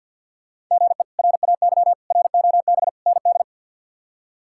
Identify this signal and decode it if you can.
Morse code, heard as a keyed tone, about 52 wpm, 690 Hz. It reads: GE RAY ROB ND
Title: K1USN SST QSO Practice - 50wpm